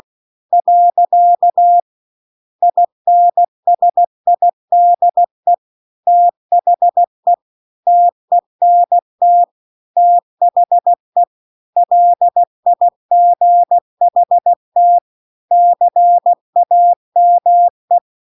Morse code, heard as a keyed tone, about 16 wpm, 700 Hz. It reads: . INSIDE THE TENT THE LIGHT CAME